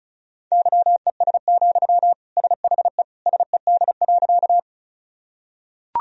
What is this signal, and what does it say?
Morse code, heard as a keyed tone, about 34 wpm, 700 Hz. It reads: YES, SHE SED.
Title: Sentences from Top 500 Words and Abbreviations 35wpm